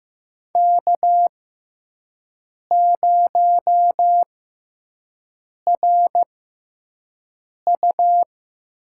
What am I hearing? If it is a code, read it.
Morse code, heard as a keyed tone, about 15 wpm, spaced as for 5 wpm, 700 Hz. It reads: K0RU
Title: USA Call Signs - 6x Character Spacing - 15wpm